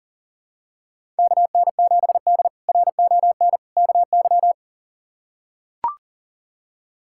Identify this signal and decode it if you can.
Morse code, heard as a keyed tone, about 40 wpm, 700 Hz. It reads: KN7D RON KY
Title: K1USN SST QSO Practice - 40wpm